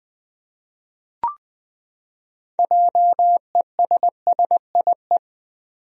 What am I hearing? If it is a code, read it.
Morse code, heard as a keyed tone, about 20 wpm, 700 Hz. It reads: JESSIE
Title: USA Common Names - 20wpm